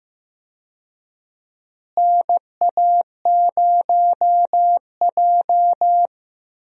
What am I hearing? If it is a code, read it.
Morse code, heard as a keyed tone, about 15 wpm, 700 Hz. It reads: NA0J
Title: USA Call Signs - 4 Characters - Repeated 2x - 15wpm